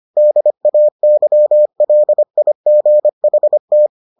Morse code, heard as a keyed tone, about 25 wpm, 600 Hz. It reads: DAYLIGHT